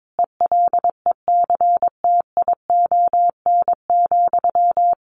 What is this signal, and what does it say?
Morse code, heard as a keyed tone, about 22 wpm, 700 Hz. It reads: ELECTION,
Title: Sentences from Top 1500 Words 22wpm